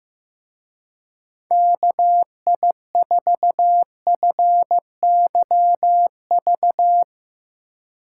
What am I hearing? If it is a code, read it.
Morse code, heard as a keyed tone, about 16 wpm, 700 Hz. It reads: KI4FYV
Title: Speed Racing - USA Call Signs 15wpm